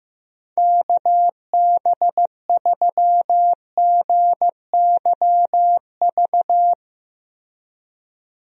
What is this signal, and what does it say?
Morse code, heard as a keyed tone, about 16 wpm, 700 Hz. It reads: KB3GYV
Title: USA Call Signs - 6 Characters - Repeated 2x - 15wpm